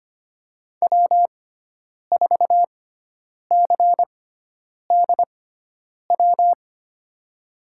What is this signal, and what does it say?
Morse code, heard as a keyed tone, about 25 wpm, spaced as for 9 wpm, 700 Hz. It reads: W4CDW
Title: USA Call Signs - 5 Characters - 6x Character Spacing - 25wpm